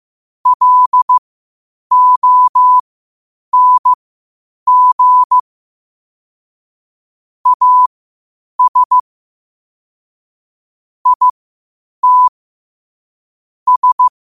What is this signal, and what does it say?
Morse code, heard as a keyed tone, about 15 wpm, spaced as for 8 wpm, 1000 Hz. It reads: LONG AS IT S
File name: qotdfiles / 8wpm